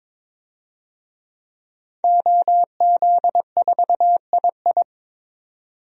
Morse code, heard as a keyed tone, about 23 wpm, 700 Hz. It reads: OZ4II
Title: DXCC-100 Call Signs - 5 Characters - 22wpm